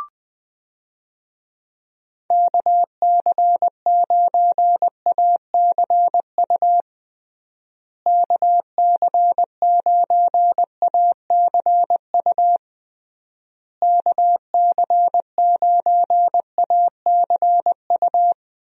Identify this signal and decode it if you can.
Morse code, heard as a keyed tone, about 20 wpm, 700 Hz. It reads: KC9ACU KC9ACU KC9ACU
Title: USA Call Signs - 6 Characters - Repeated 3x - 20wpm